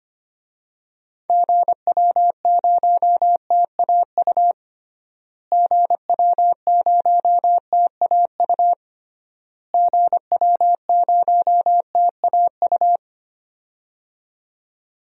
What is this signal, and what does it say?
Morse code, heard as a keyed tone, about 25 wpm, 700 Hz. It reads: GW0TAU GW0TAU GW0TAU
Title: DXCC-100 Call Signs - 6 Characters - Repeated 3x - 25wpm